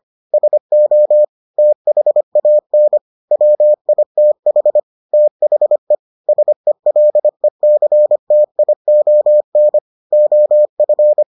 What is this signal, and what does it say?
Morse code, heard as a keyed tone, about 25 wpm, 600 Hz. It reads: SO THAN WITH THE SELECTION OF